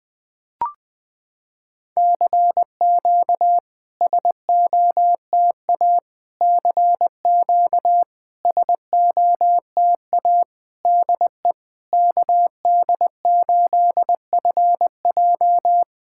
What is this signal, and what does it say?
Morse code, heard as a keyed tone, about 20 wpm, 690 Hz. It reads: CQ SOTA CQ SOTA DE KD8FJ
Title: SOTA QSO Practice 20wpm